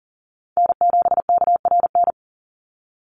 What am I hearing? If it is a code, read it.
Morse code, heard as a keyed tone, about 40 wpm, 700 Hz. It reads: N7KRN
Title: NA SOTA Chaser Callsigns 40wpm